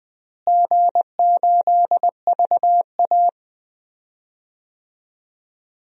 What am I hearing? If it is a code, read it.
Morse code, heard as a keyed tone, about 20 wpm, 700 Hz. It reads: G8VA
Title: DXCC-100 Call Signs - 4 Characters - Repeated 2x - 20wpm